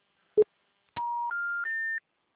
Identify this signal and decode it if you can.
Special information tone.